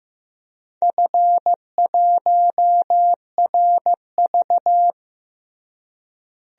Morse code, heard as a keyed tone, about 15 wpm, 700 Hz. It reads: F1RV
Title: DXCC-100 Call Signs - 4 Characters - Repeated 3x - 15wpm